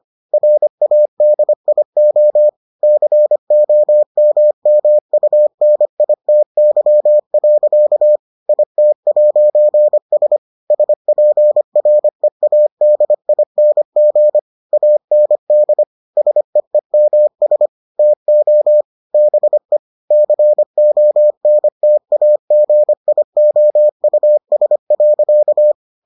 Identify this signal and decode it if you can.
Morse code, heard as a keyed tone, about 25 wpm, 600 Hz. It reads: RADIO COMMUNITY. IT'S SPREADING AND SEEMS TO BE CONTAGIOUS.